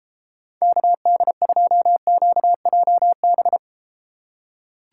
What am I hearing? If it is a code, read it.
Morse code, heard as a keyed tone, about 33 wpm, 700 Hz. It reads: KD2QJB